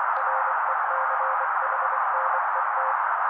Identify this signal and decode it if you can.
Morse code, heard as a keyed tone, about 24 wpm, 530 Hz. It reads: RECHNET
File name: daily / 23wpm noise